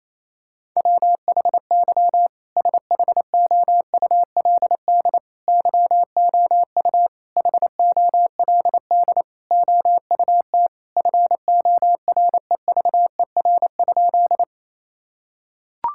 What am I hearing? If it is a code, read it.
Morse code, heard as a keyed tone, about 27 wpm, 700 Hz. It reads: WHY SHOULD YOU HOLD OUT FOREVER?
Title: Sentences from Top 1600 Words 28wpm